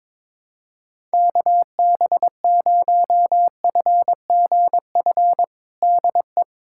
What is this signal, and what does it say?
Morse code, heard as a keyed tone, about 22 wpm, 700 Hz. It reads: KB0FGF DE
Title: USA Callsign DE Callsign - 6 Characters - 22wpm